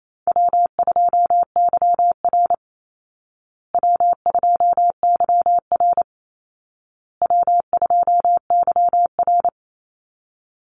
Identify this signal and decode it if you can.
Morse code, heard as a keyed tone, about 28 wpm, 700 Hz. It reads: W2YR W2YR W2YR
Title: Easy Call Signs Repeated 3x - 28wpm